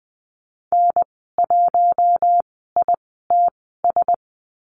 Morse code, heard as a keyed tone, about 20 wpm, spaced as for 15 wpm, 700 Hz. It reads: N1ITS
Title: 048 Lesson 9g - Number 4 - Call Signs 20wpm